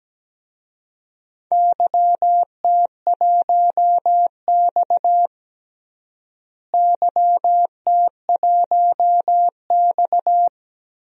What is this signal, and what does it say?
Morse code, heard as a keyed tone, about 17 wpm, 700 Hz. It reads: YT1X YT1X
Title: DXCC-100 Call Signs - 4 Characters - Repeated 2x - 17wpm